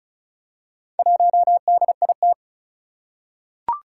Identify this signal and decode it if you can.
Morse code, heard as a keyed tone, about 36 wpm, 700 Hz. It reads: 1DIT
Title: Four Challenging Letter-Number Combinations 35wpm